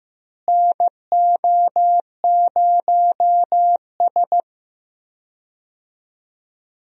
Morse code, heard as a keyed tone, about 16 wpm, 700 Hz. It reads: NO0S
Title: USA Call Signs - 4 Characters - Repeated 3x - 15wpm